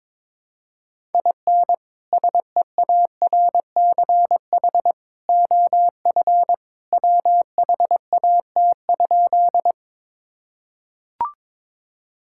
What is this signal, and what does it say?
Morse code, heard as a keyed tone, about 22 wpm, 700 Hz. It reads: IN SEARCH OF WHAT?
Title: Sentences from Top 1000 Words - Review Part 5 - 22wpm